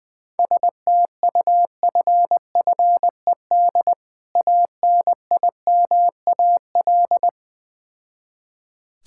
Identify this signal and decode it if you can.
Morse code, heard as a keyed tone, about 20 wpm, 690 Hz. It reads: STUFFED ANIMAL
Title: Difficult Words with Hints - 20wpm